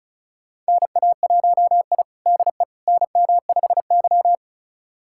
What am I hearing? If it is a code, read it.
Morse code, heard as a keyed tone, about 36 wpm, 700 Hz. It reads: NA1I DE NM5Y
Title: USA Callsign DE Callsign - 4 Characters - 35wpm